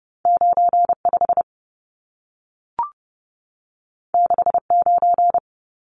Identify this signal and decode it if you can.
Morse code, heard as a keyed tone, about 30 wpm, 700 Hz. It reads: 95 69
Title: Numbers - 2-Digit 30wpm